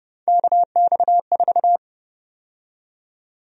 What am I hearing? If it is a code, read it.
Morse code, heard as a keyed tone, about 32 wpm, 700 Hz. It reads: KX4
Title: USA Call Sign Prefixes - 30wpm